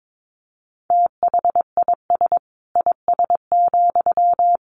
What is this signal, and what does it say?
Morse code, heard as a keyed tone, about 22 wpm, 700 Hz. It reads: THIS IS,
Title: Sentences from Top 1500 Words 22wpm